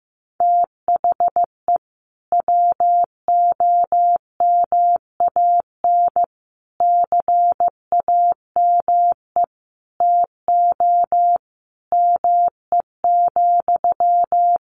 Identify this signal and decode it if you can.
Morse code, heard as a keyed tone, about 15 wpm, 700 Hz. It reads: THE WOMAN CAME TO ME,
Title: Sentences from Top 1100 Words 15wpm